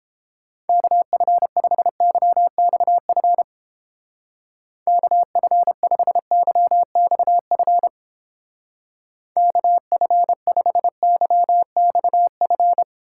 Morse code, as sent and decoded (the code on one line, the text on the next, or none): -.- ..-. ..... -.-- -..- ..-. / -.- ..-. ..... -.-- -..- ..-. / -.- ..-. ..... -.-- -..- ..-.
KF5YXF KF5YXF KF5YXF